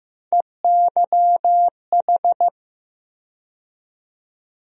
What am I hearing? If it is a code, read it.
Morse code, heard as a keyed tone, about 15 wpm, 700 Hz. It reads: EYH
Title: USA Call Sign Postfixes - 3 Letters 15wpm